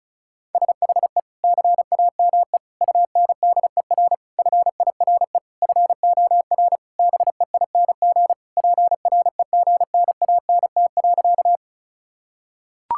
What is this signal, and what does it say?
Morse code, heard as a keyed tone, about 36 wpm, 700 Hz. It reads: SHE CAME UNDER FIRE FOR BEING PREGNANT.
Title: Sentences from Top 1000 Words - Review Part 6 - 35wpm